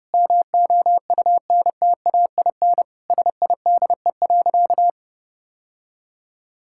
Morse code, heard as a keyed tone, about 32 wpm, 700 Hz. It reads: MOUNTAIN SIDE.
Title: Sentences from Top 1000 Words - Review Part 6 - 30wpm